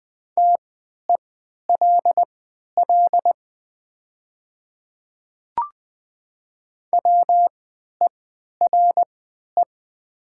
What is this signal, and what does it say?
Morse code, heard as a keyed tone, about 20 wpm, spaced as for 11 wpm, 700 Hz. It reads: TELL WERE
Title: 3x Character Spacing - Top 300 Words - 20wpm